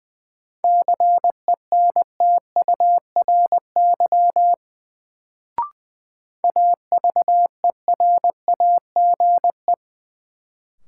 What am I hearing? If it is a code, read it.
Morse code, heard as a keyed tone, about 20 wpm, 700 Hz. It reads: CENTURY AVERAGE
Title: Sets of 5 Words - New Word Drill - 20wpm